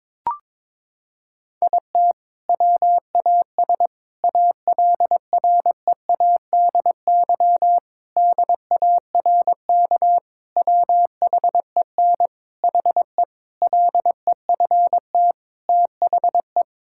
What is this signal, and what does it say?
Morse code, heard as a keyed tone, about 22 wpm, 700 Hz. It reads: IT WAS ALREADY DARK WHEN HE LEFT THE
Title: Sentences from Top 1000 Words - Review Part 5 - 22wpm